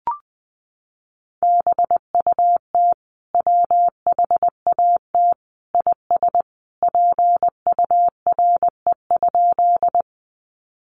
Morse code, heard as a keyed tone, about 20 wpm, 700 Hz. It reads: BUT WHAT IS PURE?
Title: Sentences from Top 1500 Words 20wpm